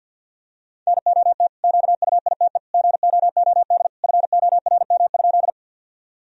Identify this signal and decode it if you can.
Morse code, heard as a keyed tone, about 48 wpm, 700 Hz. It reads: NOT QUITE GOOD FORM?
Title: Sentences from Top 1000 Words - Review Part 1 - 50wpm